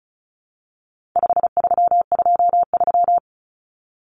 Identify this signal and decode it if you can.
Morse code, heard as a keyed tone, about 34 wpm, 700 Hz. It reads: 5323
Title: Numbers - 4-Digit 35wpm